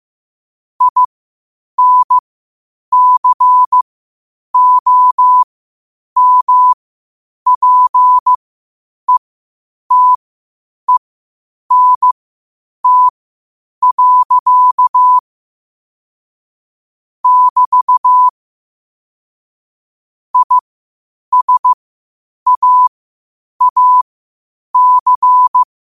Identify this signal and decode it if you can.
Morse code, heard as a keyed tone, about 15 wpm, spaced as for 8 wpm, 1000 Hz. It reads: INCOMPETENT. <BT> ISAAC
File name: 8wpm